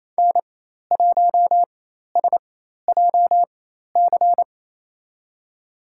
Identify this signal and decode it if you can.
Morse code, heard as a keyed tone, about 28 wpm, spaced as for 13 wpm, 700 Hz. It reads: N1SJC
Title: USA Call Signs - 5 Characters - 4x Character Spacing - 28wpm